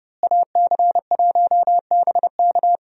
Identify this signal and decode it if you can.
Morse code, heard as a keyed tone, about 32 wpm, 700 Hz. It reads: AC1BK
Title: USA Call Signs - 5 Characters - Repeated 3x - 30wpm